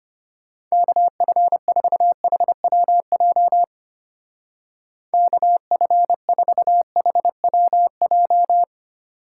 Morse code, heard as a keyed tone, about 31 wpm, 700 Hz. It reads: KF4HWJ KF4HWJ
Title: Speed Racing - USA 2x3 Call Signs 25wpm